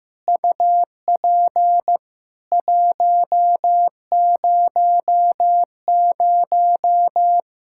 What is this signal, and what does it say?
Morse code, heard as a keyed tone, about 16 wpm, 700 Hz. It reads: UP 100
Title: Practice QSO - Exchange 2 - Info - 15wpm